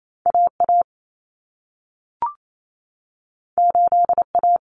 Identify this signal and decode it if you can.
Morse code, heard as a keyed tone, about 28 wpm, 700 Hz. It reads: AA 8A
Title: Common Cut Numbers - 2-Digit 28wpm